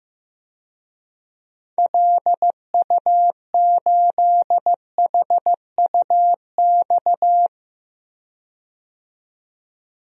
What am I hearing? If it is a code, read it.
Morse code, heard as a keyed tone, about 15 wpm, 700 Hz. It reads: LU8HUX
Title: DXCC-100 Call Signs - 6 Characters - 15wpm